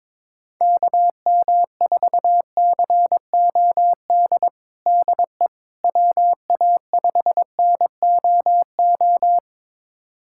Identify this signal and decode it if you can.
Morse code, heard as a keyed tone, about 22 wpm, 700 Hz. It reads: KM4COD DE WA5NOO
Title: USA Callsign DE Callsign - 6 Characters - 22wpm